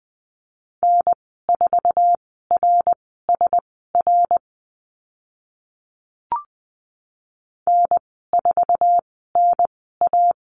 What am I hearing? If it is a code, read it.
Morse code, heard as a keyed tone, about 20 wpm, spaced as for 14 wpm, 700 Hz. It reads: N4RSR N4NA
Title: 062 Lesson 11g - Letter h - Call Signs 20wpm